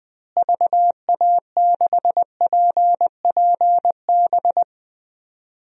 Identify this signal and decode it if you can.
Morse code, heard as a keyed tone, about 20 wpm, 700 Hz. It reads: VA6PPB
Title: DXCC-100 Call Signs - 6 Characters - Repeated 2x - 20wpm